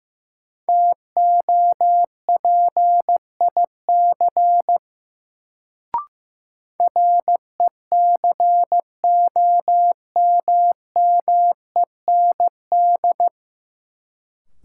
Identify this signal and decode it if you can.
Morse code, heard as a keyed tone, about 15 wpm, 700 Hz. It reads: TOPIC RECOMMEND
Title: Sentences from Top 1200 Words - New Word Drill - 15wpm